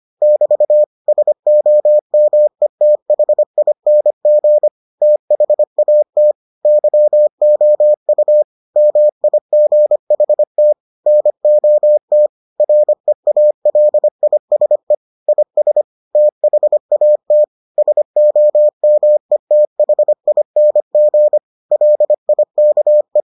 Morse code, heard as a keyed tone, about 25 wpm, 600 Hz. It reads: <BT> SOMETHING THAT YOU MIGHT NOT REALISE IS THAT SOMETHING LIKE